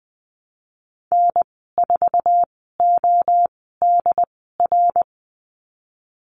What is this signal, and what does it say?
Morse code, heard as a keyed tone, about 20 wpm, spaced as for 15 wpm, 700 Hz. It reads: N4ODR
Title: 128 Lesson 21g - Number 6 - Call Signs 20wpm